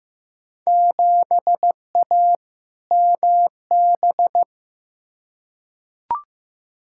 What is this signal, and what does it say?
Morse code, heard as a keyed tone, about 15 wpm, 700 Hz. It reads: 7A MB
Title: Field Day Exchanges Encore - 15wpm